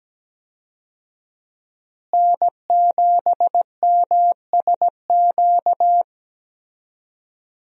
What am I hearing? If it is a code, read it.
Morse code, heard as a keyed tone, about 17 wpm, 700 Hz. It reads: N7MSQ
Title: USA Call Signs - 5 Characters - Repeated 3x - 17wpm